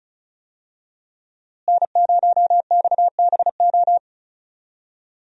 Morse code, heard as a keyed tone, about 36 wpm, 700 Hz. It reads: N0XBO
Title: USA Call Signs - 5 Characters - Repeated 2x - 35wpm